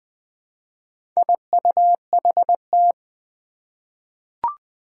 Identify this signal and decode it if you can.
Morse code, heard as a keyed tone, about 20 wpm, 700 Hz. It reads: IUHT
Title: Four Challenging Letter-Number Combinations 20wpm